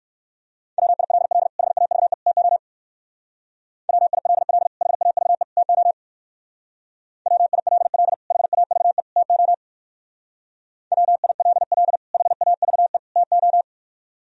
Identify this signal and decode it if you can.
Morse code, heard as a keyed tone, about 60 wpm, 700 Hz. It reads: WILL HAVE TO WILL HAVE TO WILL HAVE TO WILL HAVE TO